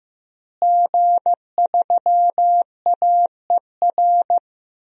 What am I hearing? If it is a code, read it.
Morse code, heard as a keyed tone, about 15 wpm, 700 Hz. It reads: G3AER
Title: DXCC-100 Call Signs - 5 Characters - Repeated 3x - 15wpm